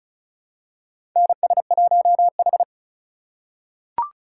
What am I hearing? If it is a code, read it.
Morse code, heard as a keyed tone, about 36 wpm, 700 Hz. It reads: NS1H
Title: USA Call Signs - 4 Characters - 3x Character Spacing - 35wpm